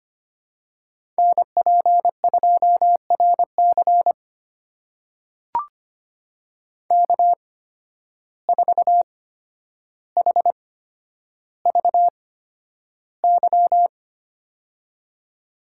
Morse code, heard as a keyed tone, about 25 wpm, spaced as for 11 wpm, 700 Hz. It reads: NP2RC K4HVY
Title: USA Call Signs - 5 Characters - 8x Character Spacing - 25wpm